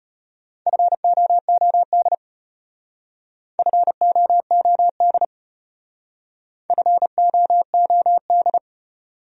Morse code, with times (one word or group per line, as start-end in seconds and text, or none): 0.7-2.1 FOOD
3.6-5.2 FOOD
6.7-8.6 FOOD